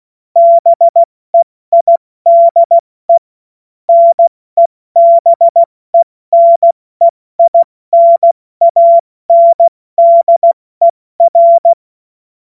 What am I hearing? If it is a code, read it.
Morse code, heard as a keyed tone, about 16 wpm, 680 Hz. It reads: BEIDE NEBENEINANDER
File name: Morsekurs / Witze / Witz 15. Februar 2022 70 F80